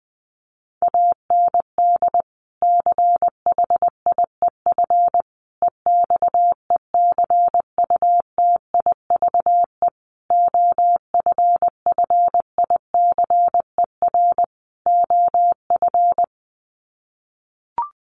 Morse code, heard as a keyed tone, about 20 wpm, 700 Hz. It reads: AND CHIEF EXECUTIVE OFFICER OF
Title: Sets of 5 Words Encore2 20wpm